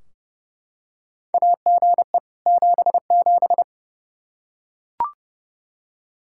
Morse code, heard as a keyed tone, about 31 wpm, 700 Hz. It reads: AGE 77
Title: Practice QSO - Exchange 3 - More Info - 30wpm